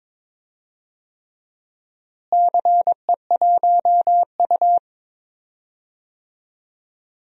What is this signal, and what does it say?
Morse code, heard as a keyed tone, about 21 wpm, 700 Hz. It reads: CE1U
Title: DXCC-100 Call Signs - 4 Characters - Repeated 3x - 22wpm